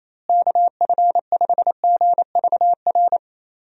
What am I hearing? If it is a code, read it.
Morse code, heard as a keyed tone, about 27 wpm, 700 Hz. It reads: KF5GVR
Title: USA Call Signs - 6 Characters - Repeated 3x - 28wpm